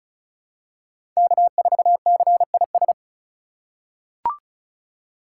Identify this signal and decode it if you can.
Morse code, heard as a keyed tone, about 36 wpm, 700 Hz. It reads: K4CIS
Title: USA Call Signs - 5 Characters - 4x Character Spacing - 35wpm